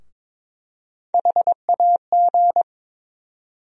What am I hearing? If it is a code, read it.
Morse code, heard as a keyed tone, about 22 wpm, 700 Hz. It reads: HAG
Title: USA Call Sign Postfixes - 3 Letters 22wpm